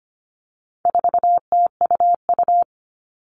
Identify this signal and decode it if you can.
Morse code, heard as a keyed tone, about 25 wpm, 700 Hz. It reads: 4TUU
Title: All Cut Numbers - 4-Digit 25wpm